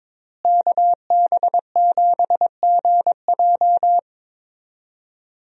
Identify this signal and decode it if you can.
Morse code, heard as a keyed tone, about 22 wpm, 700 Hz. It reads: KB7GJ
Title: USA Call Signs - 5 Characters - Repeated 3x - 22wpm